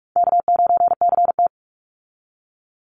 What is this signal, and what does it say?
Morse code, heard as a keyed tone, about 46 wpm, 700 Hz. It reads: K9CT
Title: Easy Call Signs - 45wpm